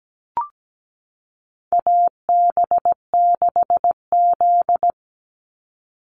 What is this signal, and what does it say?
Morse code, heard as a keyed tone, about 17 wpm, 700 Hz. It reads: AB6Z
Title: Call Signs Repeated 2x - 17wpm